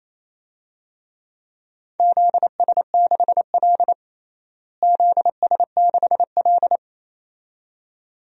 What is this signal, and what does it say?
Morse code, heard as a keyed tone, about 28 wpm, 700 Hz. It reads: ZS6L ZS6L
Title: DXCC-100 Call Signs - 4 Characters - Repeated 2x - 28wpm